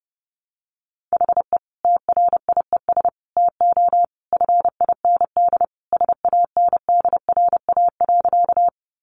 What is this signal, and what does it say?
Morse code, heard as a keyed tone, about 30 wpm, 700 Hz. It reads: HE TRIES TO FIND SANDRA.
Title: Easy Phrases 30wpm